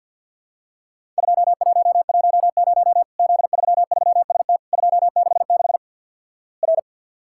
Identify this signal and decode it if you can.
Morse code, heard as a keyed tone, about 50 wpm, 700 Hz. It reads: 2110 Z33ST 266 R
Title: Worked All Europe DX Contest - QTC Exchange - 50wpm